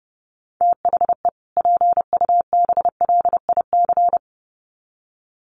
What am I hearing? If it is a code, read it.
Morse code, heard as a keyed tone, about 30 wpm, 700 Hz. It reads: THE PUBLIC
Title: Sets of 2 Words 30wpm